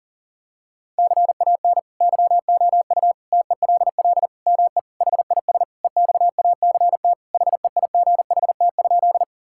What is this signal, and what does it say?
Morse code, heard as a keyed tone, about 40 wpm, 700 Hz. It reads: CAN YOU TELL ME HIS EXACT HEIGHT?
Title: Sentences from Top 1600 Words 40wpm